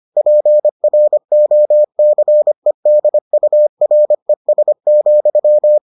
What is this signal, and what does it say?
Morse code, heard as a keyed tone, about 25 wpm, 600 Hz. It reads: PROCEDURES,